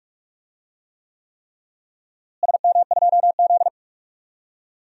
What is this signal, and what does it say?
Morse code, heard as a keyed tone, about 46 wpm, 690 Hz. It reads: SM2Z
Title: DXCC-100 Call Signs - 4 Characters - Repeated 3x - 45wpm